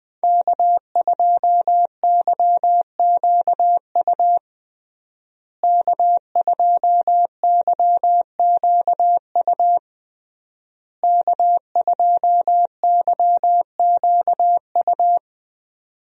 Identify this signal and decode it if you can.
Morse code, heard as a keyed tone, about 20 wpm, 700 Hz. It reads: K2YQU K2YQU K2YQU
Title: USA Call Signs - 5 Characters - Repeated 3x - 20wpm